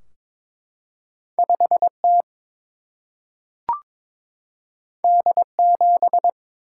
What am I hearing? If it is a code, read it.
Morse code, heard as a keyed tone, about 22 wpm, 700 Hz. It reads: 5 T D 7
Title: Single-Letter Single-Number 22wpm